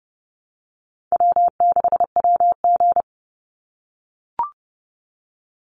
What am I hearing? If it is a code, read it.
Morse code, heard as a keyed tone, about 30 wpm, 700 Hz. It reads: W6WG
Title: Easy Call Signs - 30wpm